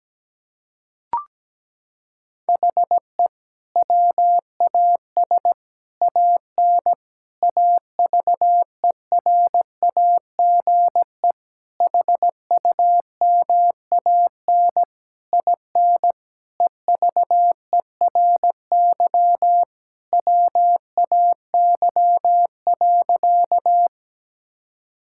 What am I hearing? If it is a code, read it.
Morse code, heard as a keyed tone, about 17 wpm, 700 Hz. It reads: HE WAS AN AVERAGE HUMAN IN EVERY WAY.
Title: Sentences from Top 1000 Words - Review Part 6 - 17wpm